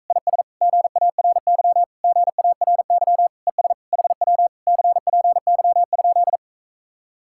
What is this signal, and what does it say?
Morse code, heard as a keyed tone, about 44 wpm, 700 Hz. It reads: IS GARY GARY ES HW CPY?